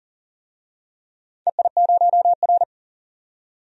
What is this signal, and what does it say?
Morse code, heard as a keyed tone, about 39 wpm, 700 Hz. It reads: EI0R
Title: DXCC-100 Call Signs - 4 Characters - 40wpm